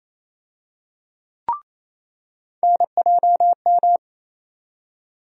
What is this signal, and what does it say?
Morse code, heard as a keyed tone, about 28 wpm, 700 Hz. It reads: NJM
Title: USA Call Sign Postfixes - 3 Letters 28wpm